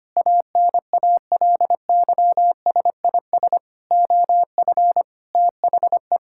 Morse code, heard as a keyed tone, about 25 wpm, 700 Hz. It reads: ANALYSIS OF THE
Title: Sentences from Top 1000 Words - Review Part 5 - 25wpm